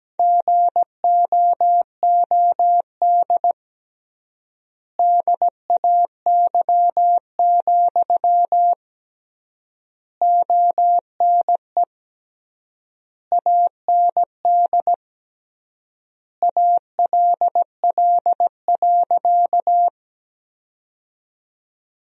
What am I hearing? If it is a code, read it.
Morse code, heard as a keyed tone, about 17 wpm, 700 Hz. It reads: GOOD DAY, ONE AND ALL.
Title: 3x Word-Spacing - Sentences from Top 200 Words - 17wpm